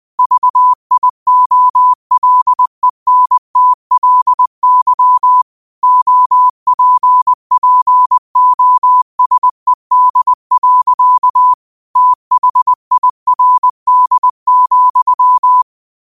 Morse code, heard as a keyed tone, about 20 wpm, 1000 Hz. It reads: VIOLENTLY OPPOSED. THIRD,
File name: qotdfiles / 20wpm